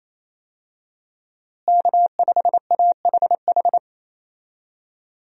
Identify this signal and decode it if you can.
Morse code, heard as a keyed tone, about 28 wpm, 700 Hz. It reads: K5AHH
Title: USA Call Signs - 5 Characters - Repeated 3x - 28wpm